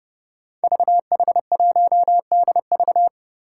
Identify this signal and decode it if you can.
Morse code, heard as a keyed tone, about 30 wpm, 700 Hz. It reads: VH1DV
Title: Five Challenging Letter-Number Combinations 30wpm